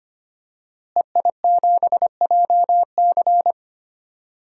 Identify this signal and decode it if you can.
Morse code, heard as a keyed tone, about 25 wpm, 700 Hz. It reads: EI7JC
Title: DXCC-100 Call Signs - 5 Characters - Repeated 2x - 25wpm